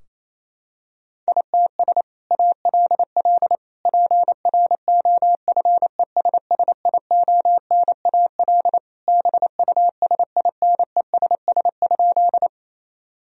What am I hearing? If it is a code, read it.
Morse code, heard as a keyed tone, about 27 wpm, 700 Hz. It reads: ITS ALL PROFESSIONAL BUSINESS?
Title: Sentences from Top 1000 Words - Review Part 4 - 28wpm